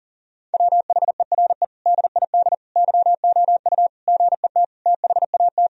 Morse code, heard as a keyed tone, about 40 wpm, 700 Hz. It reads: WHERE DID YOU GET THAT
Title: Sentences from Top 1000 Words - Review Part 3 - 40wpm